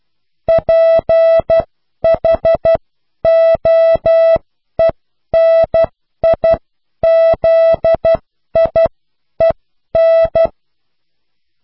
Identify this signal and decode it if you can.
Morse code, heard as a keyed tone, about 12 wpm, spaced as for 10 wpm, 660 Hz. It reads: PHOENIZIEN
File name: cw / pseudo worden(10)